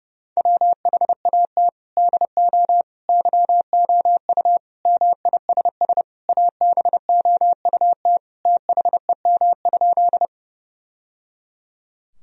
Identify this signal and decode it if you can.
Morse code, heard as a keyed tone, about 30 wpm, 700 Hz. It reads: WHAT DO YOU MISS ABOUT THEM?
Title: Sentences from Top 1000 Words - Review Part 3 - 30wpm